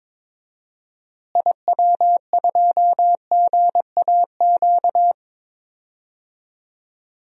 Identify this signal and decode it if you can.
Morse code, heard as a keyed tone, about 22 wpm, 700 Hz. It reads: IW2GAQ
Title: DXCC-100 Call Signs - 6 Characters - Repeated 3x - 22wpm